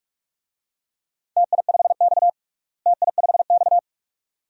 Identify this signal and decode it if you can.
Morse code, heard as a keyed tone, about 46 wpm, 700 Hz. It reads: TI5X TI5X
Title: DXCC-100 Call Signs - 4 Characters - Repeated 2x - 45wpm